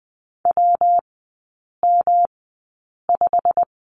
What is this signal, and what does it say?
Morse code, heard as a keyed tone, about 20 wpm, spaced as for 8 wpm, 690 Hz. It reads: WM5
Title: 133 Lesson 22e - Question Mark - Three Characters 20wpm